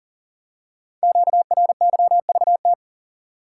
Morse code, heard as a keyed tone, about 38 wpm, 700 Hz. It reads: QRYVT
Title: Code Groups - 5 Letter:Numbers - 40wpm